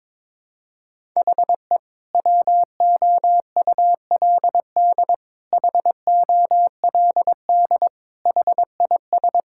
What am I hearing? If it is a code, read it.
Morse code, heard as a keyed tone, about 22 wpm, 700 Hz. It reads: HE WOULD HOLD HIS
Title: Sentences from Top 1000 Words - Review Part 6 - 22wpm